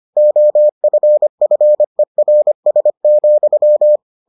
Morse code, heard as a keyed tone, about 25 wpm, 600 Hz. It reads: OFFERS,